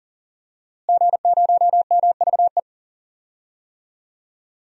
Morse code, heard as a keyed tone, about 40 wpm, 700 Hz. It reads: G0MVE
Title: DXCC-100 Call Signs - 5 Characters - Repeated 3x - 40wpm